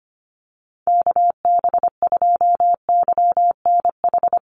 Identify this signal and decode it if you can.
Morse code, heard as a keyed tone, about 25 wpm, 700 Hz. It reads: KB2YNH
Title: USA Call Signs Repeated 3x 25wpm